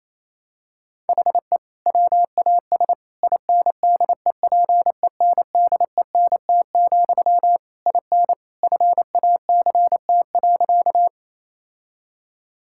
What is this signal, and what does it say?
Morse code, heard as a keyed tone, about 27 wpm, 700 Hz. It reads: HE WAS INDEPENDENT, IN FACT.
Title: Sentences from Top 1000 Words - Review Part 6 - 28wpm